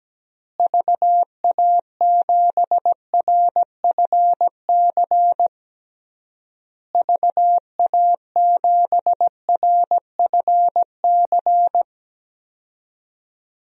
Morse code, heard as a keyed tone, about 17 wpm, 700 Hz. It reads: VA7RFC VA7RFC
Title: DXCC-100 Call Signs - 6 Characters - Repeated 3x - 17wpm